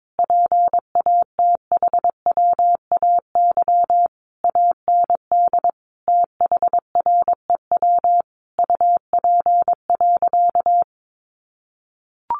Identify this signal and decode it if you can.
Morse code, heard as a keyed tone, about 22 wpm, 700 Hz. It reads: PATHWAY AND THREW UP.
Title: A Sound Of Thunder 22wpm